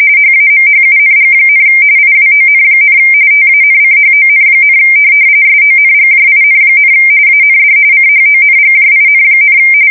RTTY, about 45.45 baud. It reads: HAMHUNG, IWON, TANCHON, SONGJIN, AND CHONGJIN,
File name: fsk170 / USS-New-Jersey